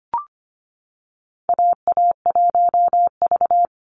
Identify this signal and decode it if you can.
Morse code, heard as a keyed tone, about 25 wpm, 700 Hz. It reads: AA1V
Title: Call Signs - 25wpm